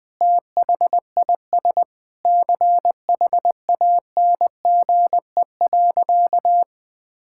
Morse code, heard as a keyed tone, about 20 wpm, 700 Hz. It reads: THIS CHANGE.
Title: Sentences with 8-Word Phrases - Spoken Beginning - 20wpm